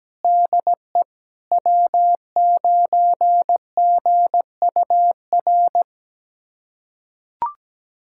Morse code, heard as a keyed tone, about 17 wpm, 700 Hz. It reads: DE W9GUR
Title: USA Callsign DE Callsign - 5 Characters - 17wpm